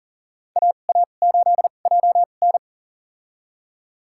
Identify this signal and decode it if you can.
Morse code, heard as a keyed tone, about 40 wpm, spaced as for 29 wpm, 700 Hz. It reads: AA8JN
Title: USA Call Signs - 5 Characters - 2x Character Spacing - 40wpm